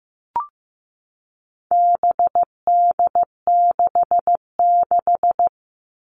Morse code, heard as a keyed tone, about 15 wpm, 700 Hz. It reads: BD66
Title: All Cut Numbers - 4-Digit 15wpm